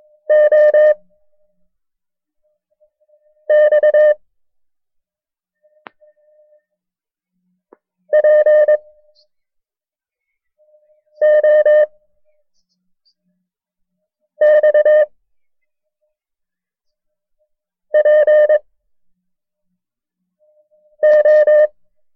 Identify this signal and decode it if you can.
Morse code, heard as a keyed tone, about 22 wpm, 610 Hz. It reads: O X P O X P O
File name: FreeFiles / CW-practice / KB6MT-tape1-side2